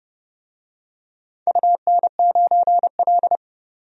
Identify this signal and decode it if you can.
Morse code, heard as a keyed tone, about 30 wpm, 700 Hz. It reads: UN9L
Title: DXCC-100 Call Signs - 4 Characters - 30wpm